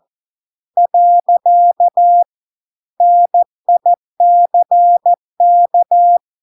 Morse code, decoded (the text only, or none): . NICK